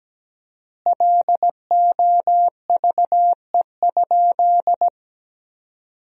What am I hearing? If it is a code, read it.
Morse code, heard as a keyed tone, about 17 wpm, 700 Hz. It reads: LOVE?
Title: 4x Word-Spacing - Sentences from Top 500 Words - 17wpm